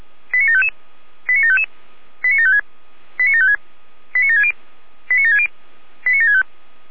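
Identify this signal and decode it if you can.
Five-tone selective call (ZVEI), heard as quick runs of five tones, tones of about 70 ms. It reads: 89755 89755 89756 89756 89760 89760 89765
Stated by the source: Five-tone ZVEI call received on an SDR.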